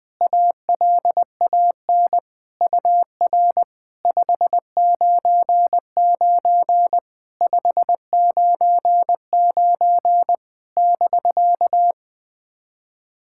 Morse code, heard as a keyed tone, about 20 wpm, 700 Hz. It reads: ALAN UR 599 599 <BK>
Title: POTA QSO Practice - 20wpm